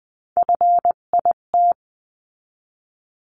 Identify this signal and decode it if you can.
Morse code, heard as a keyed tone, about 20 wpm, 700 Hz. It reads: FIT
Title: 258 Lesson 42c - Words 16-20wpm